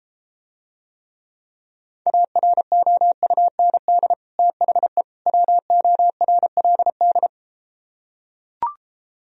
Morse code, heard as a keyed tone, about 33 wpm, 700 Hz. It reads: AROUND THE WORLD